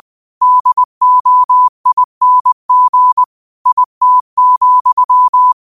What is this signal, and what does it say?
Morse code, heard as a keyed tone, about 20 wpm, 1000 Hz. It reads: DOING IT,
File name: qotdfiles / 20wpm